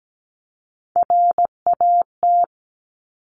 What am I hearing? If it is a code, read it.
Morse code, heard as a keyed tone, about 17 wpm, 700 Hz. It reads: RAT
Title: Top Three Letter Combinations in Words 17wpm